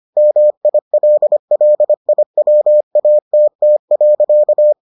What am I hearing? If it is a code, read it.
Morse code, heard as a keyed tone, about 25 wpm, 600 Hz. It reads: MILLIWATT.